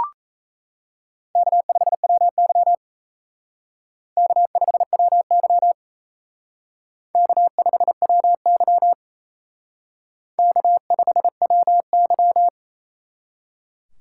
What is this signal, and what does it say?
Morse code, heard as a keyed tone, about 35 wpm, 700 Hz. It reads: K5WY K5WY K5WY K5WY
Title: Speed Racing - USA Call Signs 28wpm